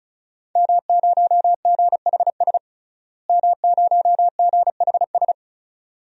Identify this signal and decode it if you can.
Morse code, heard as a keyed tone, about 36 wpm, 700 Hz. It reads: M0GHS M0GHS
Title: DXCC-100 Call Signs - 5 Characters - Repeated 2x - 35wpm